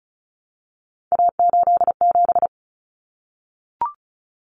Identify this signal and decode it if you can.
Morse code, heard as a keyed tone, about 34 wpm, 700 Hz. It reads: A87
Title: Common Cut Numbers - 3-Digit 35wpm